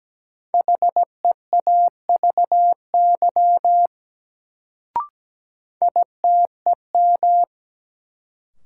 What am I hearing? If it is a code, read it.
Morse code, heard as a keyed tone, about 17 wpm, 700 Hz. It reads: HEAVY ITEM
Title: Sentences from Top 1000 Words - Review Part 6 - New Word Drill - 17wpm